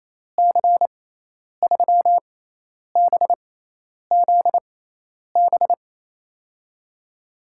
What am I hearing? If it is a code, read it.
Morse code, heard as a keyed tone, about 28 wpm, spaced as for 10 wpm, 700 Hz. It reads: C3BZB
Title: USA Call Signs - 6x Character Spacing - 28wpm